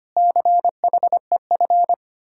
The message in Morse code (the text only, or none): CHEF